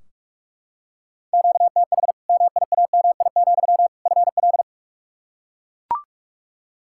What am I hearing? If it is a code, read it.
Morse code, heard as a keyed tone, about 43 wpm, 700 Hz. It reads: QTH MIAMI, FL
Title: Practice QSO - Exchange 1 - Beginning - 45wpm